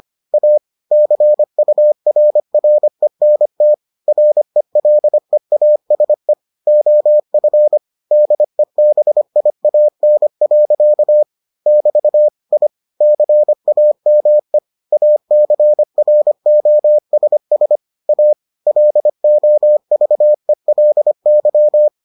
Morse code, heard as a keyed tone, about 25 wpm, 600 Hz. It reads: A CURRENT RELEASE OF DEBIAN. <BT> I CAME ACROSS A LOVELY